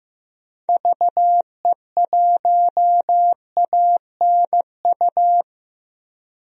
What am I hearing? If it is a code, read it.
Morse code, heard as a keyed tone, about 14 wpm, 700 Hz. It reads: VE1ANU
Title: DXCC-100 Call Signs - 6 Characters - 15wpm